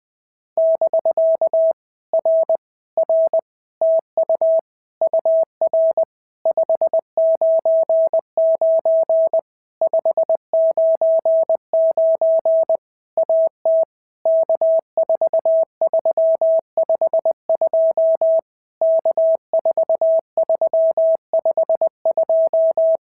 Morse code, heard as a keyed tone, about 20 wpm, 650 Hz. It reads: <BK> R R TU UR 599 599 AT K4352 K4352
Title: POTA QSO Practice - 20wpm